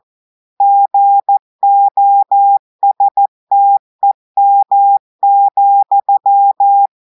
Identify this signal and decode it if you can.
Morse code, heard as a keyed tone, about 14 wpm, 800 Hz. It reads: GOSTEM,
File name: audios / portugais / portugais texte en clair 2 vitesse 14 mots minutes